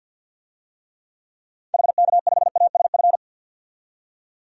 Morse code, heard as a keyed tone, about 52 wpm, 700 Hz. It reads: HK5ASF
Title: DXCC-100 Call Signs - 6 Characters - 50wpm